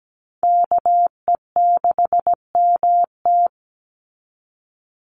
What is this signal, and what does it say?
Morse code, heard as a keyed tone, about 17 wpm, 700 Hz. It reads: KE6MT
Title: NA SOTA Chaser Callsigns 17wpm